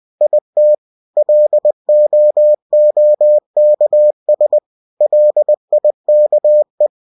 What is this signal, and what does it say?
Morse code, heard as a keyed tone, about 20 wpm, 600 Hz. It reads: IT LOOKS LIKE